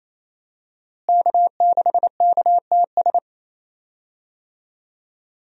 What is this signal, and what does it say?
Morse code, heard as a keyed tone, about 28 wpm, 700 Hz. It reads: K6KTS
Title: USA Call Signs - 5 Characters - Repeated 2x - 28wpm